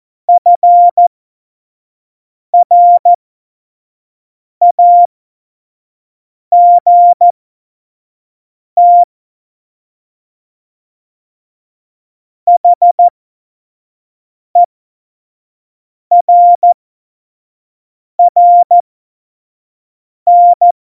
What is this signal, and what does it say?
Morse code, heard as a keyed tone, about 14 wpm, spaced as for 5 wpm, 700 Hz. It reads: FRAGT HERRN
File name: Morsekurs / Witze / Witz 01. September 2020, 25 F70